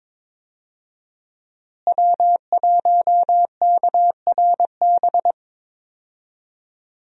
Morse code, heard as a keyed tone, about 22 wpm, 700 Hz. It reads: W1KRB
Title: USA Call Signs - 5 Characters - Repeated 2x - 22wpm